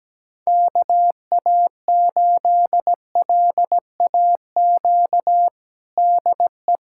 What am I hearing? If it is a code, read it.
Morse code, heard as a keyed tone, about 17 wpm, 700 Hz. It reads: KA8LAQ DE
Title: USA Callsign DE Callsign 17wpm